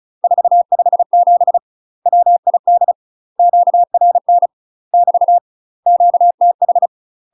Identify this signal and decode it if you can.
Morse code, heard as a keyed tone, about 34 wpm, 700 Hz. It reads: 457 WID QRN <BT> QTH
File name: UK MASTER QSOs TRAINING / 35WPM / UK MASTER QSOs 0005